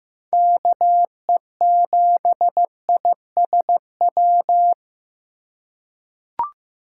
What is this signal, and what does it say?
Morse code, heard as a keyed tone, about 15 wpm, 700 Hz. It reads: KE7ISW
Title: USA Call Signs - 6 Characters - 3x Character Spacing - 15wpm